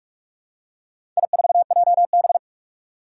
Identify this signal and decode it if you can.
Morse code, heard as a keyed tone, about 46 wpm, 700 Hz. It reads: I4JB
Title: Four Challenging Letter-Number Combinations 45wpm